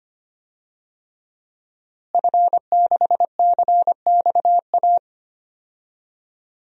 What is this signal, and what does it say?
Morse code, heard as a keyed tone, about 25 wpm, 700 Hz. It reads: F6CXA
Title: DXCC-100 Call Signs - 5 Characters - Repeated 3x - 25wpm